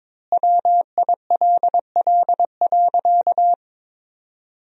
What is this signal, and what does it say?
Morse code, heard as a keyed tone, about 22 wpm, 700 Hz. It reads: WILL.
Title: 4x Word-Spacing - Sentences from Top 100 Words - 22wpm